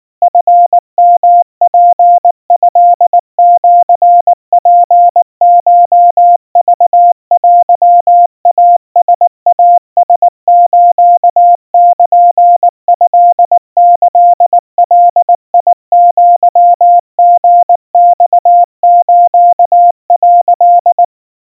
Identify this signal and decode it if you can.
Morse code (Japanese Wabun code), heard as a keyed tone, about 19 wpm, 700 Hz. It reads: チヨツトシツコクテイライラスルトキガアリマス」
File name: kawa / Morse sample / kamyu 70 cpm 700 Hz Crispy